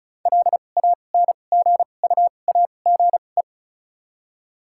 Morse code, heard as a keyed tone, about 36 wpm, spaced as for 25 wpm, 700 Hz. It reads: LANGUAGE
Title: 2x Character Spacing - Top 500 Words - 35wpm